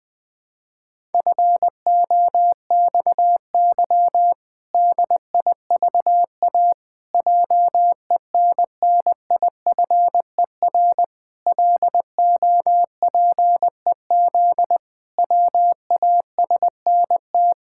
Morse code, heard as a keyed tone, about 21 wpm, 700 Hz. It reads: FOXY DIVA JENNIFER LOPEZ WASNT
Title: Pangram Challenge v2 - 20wpm